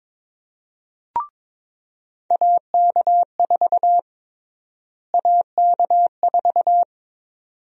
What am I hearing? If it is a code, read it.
Morse code, heard as a keyed tone, about 22 wpm, 700 Hz. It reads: AK4 AK4
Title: USA Call Sign Prefixes - Repeated 2x - 22wpm